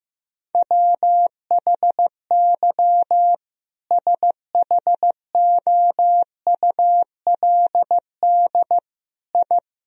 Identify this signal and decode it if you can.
Morse code, heard as a keyed tone, about 15 wpm, 700 Hz. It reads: WHY SHOULD I
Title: Sentences from Top 1000 Words - Review Part 3 - 15wpm